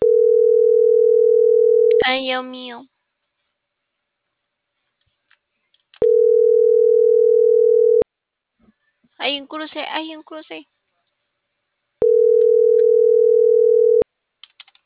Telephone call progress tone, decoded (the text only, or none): ringback tone